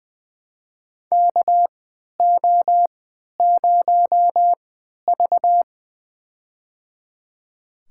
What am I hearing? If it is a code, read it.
Morse code, heard as a keyed tone, about 20 wpm, spaced as for 11 wpm, 700 Hz. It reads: KO0V
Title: USA Call Signs - 3x Character Spacing - 20wpm